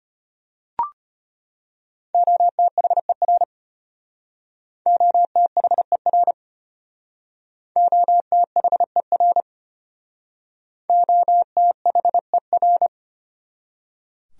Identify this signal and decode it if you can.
Morse code, heard as a keyed tone, about 31 wpm, 700 Hz. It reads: OTHER OTHER OTHER OTHER
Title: Speed-Racing Top 500 Words 25wpm